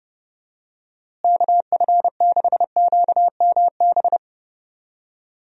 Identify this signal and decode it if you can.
Morse code, heard as a keyed tone, about 32 wpm, 700 Hz. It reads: KF6QMB
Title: USA Call Signs - 6 Characters - Repeated 3x - 30wpm